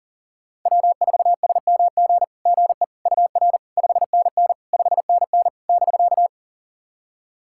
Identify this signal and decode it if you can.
Morse code, heard as a keyed tone, about 43 wpm, 700 Hz. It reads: W4SMG GE UR 5NN 5NN <BK>
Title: SOTA QSO Practice 40wpm